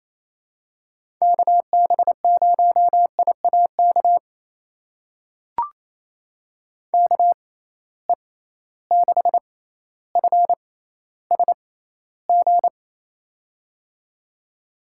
Morse code, heard as a keyed tone, about 28 wpm, spaced as for 14 wpm, 700 Hz. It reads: KB0IAK KE6FSG
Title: USA Call Signs - 6 Characters - 6x Character Spacing - 28wpm